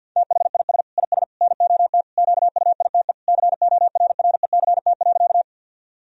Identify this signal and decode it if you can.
Morse code, heard as a keyed tone, about 49 wpm, 700 Hz. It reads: THIS IS NOT QUITE CORRECT.
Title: Sentences from Top 1000 Words - Review Part 2 - 50wpm